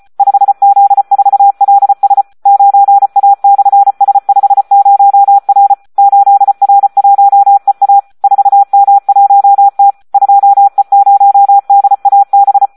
Morse code, heard as a keyed tone, about 34 wpm, 800 Hz, with radio noise. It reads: 5Z4LS 9A/S50R 8R1EA 4M1T 2E0DAB
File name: Audio practica CW / SD 1 34